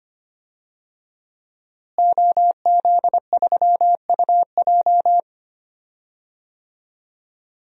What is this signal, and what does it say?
Morse code, heard as a keyed tone, about 25 wpm, 700 Hz. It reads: OZ3UJ
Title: DXCC-100 Call Signs - 5 Characters - Repeated 2x - 25wpm